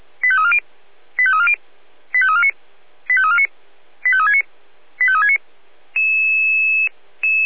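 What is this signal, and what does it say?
Five-tone selective call (ZVEI), heard as quick runs of five tones, tones of about 70 ms. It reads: 85340 85340 85349 85349 85369 85369
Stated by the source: ZVEI five-tone selective call received on an SDR.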